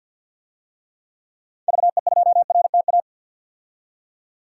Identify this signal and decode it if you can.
Morse code, heard as a keyed tone, about 50 wpm, 700 Hz. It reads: VE2RTA
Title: DXCC-100 Call Signs - 6 Characters - 50wpm